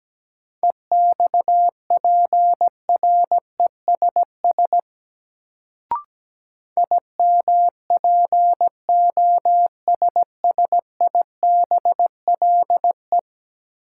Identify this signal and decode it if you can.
Morse code, heard as a keyed tone, about 17 wpm, 700 Hz. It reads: EXPRESS IMPOSSIBLE
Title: Sentences from Top 1100 Words - New Word Drill - 17wpm